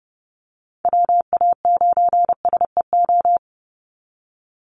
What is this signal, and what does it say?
Morse code, heard as a keyed tone, about 30 wpm, 700 Hz. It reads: WA9SEO
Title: Call Signs Repeated 3x - 30wpm